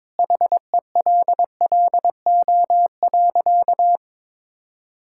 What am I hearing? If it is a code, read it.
Morse code, heard as a keyed tone, about 22 wpm, 700 Hz. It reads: HELLO.
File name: practice / 22wpm / Sentences with 2-Word Phrases - 22wpm